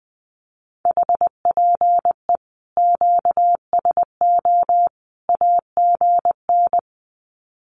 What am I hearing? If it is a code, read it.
Morse code, heard as a keyed tone, about 20 wpm, 700 Hz. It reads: HPE QSO AGN
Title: Common QSO Phrases 20wpm